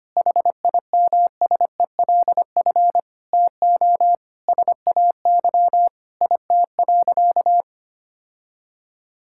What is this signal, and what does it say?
Morse code, heard as a keyed tone, about 25 wpm, 700 Hz. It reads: HIMSELF TO SAY IT.
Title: Sentences from Top 1600 Words 25wpm